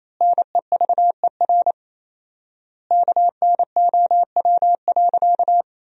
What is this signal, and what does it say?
Morse code, heard as a keyed tone, about 27 wpm, 700 Hz. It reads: NEVER KNOW.
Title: 4x Word-Spacing - Sentences from Top 300 Words - 28wpm